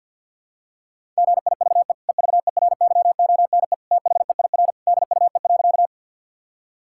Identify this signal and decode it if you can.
Morse code, heard as a keyed tone, about 50 wpm, 700 Hz. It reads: GIVE EVERYONE THEIR DUE.
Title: Sentences from Top 1000 Words - Review Part 6 - 50wpm